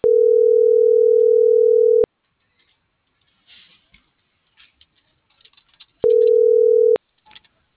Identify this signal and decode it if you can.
Ringback tone (North America), 440 + 480 Hz.
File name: CENTRAL AGUILERA / out ram.lmorena.2025 - .68919